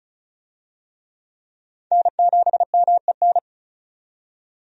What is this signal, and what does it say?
Morse code, heard as a keyed tone, about 36 wpm, 700 Hz. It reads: N7MEN
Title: USA Call Signs - 5 Characters - 35wpm